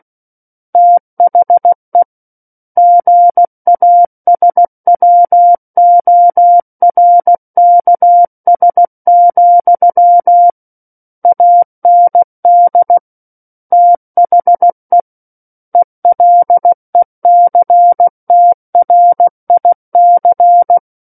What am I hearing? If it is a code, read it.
Morse code, heard as a keyed tone, about 16 wpm, 700 Hz. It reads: THE GASWORKS, AND THE ELECTRIC
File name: Book One / 07-Chapter 7 - 16 WPM